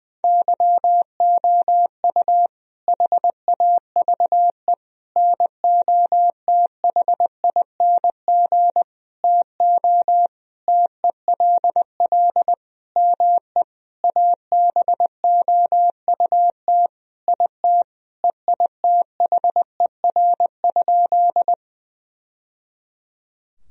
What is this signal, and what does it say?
Morse code, heard as a keyed tone, about 21 wpm, 700 Hz. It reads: YOU HAVE NOTHING TO TELL ME ABOUT IT EITHER?
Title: Sentences from Top 1000 Words - Review Part 4 - 20wpm